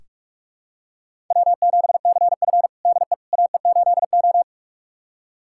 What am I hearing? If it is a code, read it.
Morse code, heard as a keyed tone, about 46 wpm, 700 Hz. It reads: W7CF DE AE8O
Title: USA Callsign DE Callsign - 4 Characters - 45wpm